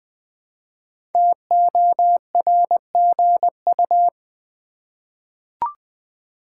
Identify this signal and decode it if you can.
Morse code, heard as a keyed tone, about 20 wpm, 700 Hz. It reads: TORGU
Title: Code Groups - 5 Letters - 20wpm